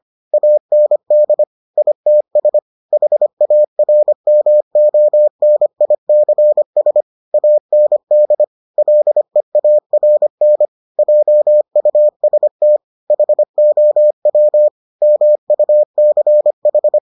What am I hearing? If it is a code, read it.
Morse code, heard as a keyed tone, about 25 wpm, 600 Hz. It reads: AND ITS HARMONICS AND LEARN JUST HOW MUCH